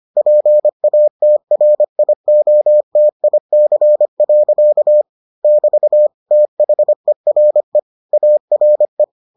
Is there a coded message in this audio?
Yes: Morse code from 0.2 to 9.0 s, heard as a keyed tone, about 25 wpm, 600 Hz. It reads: PATRIOTIC. <BT> THERE ARE